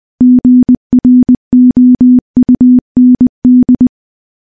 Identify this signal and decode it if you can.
Morse code, heard as a keyed tone, about 20 wpm, 250 Hz. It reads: GROUND
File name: morse / morse-training-20WPM